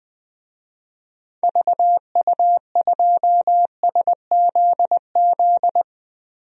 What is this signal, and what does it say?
Morse code, heard as a keyed tone, about 20 wpm, 700 Hz. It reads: VU2SZZ
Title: DXCC-100 Call Signs - 6 Characters - 20wpm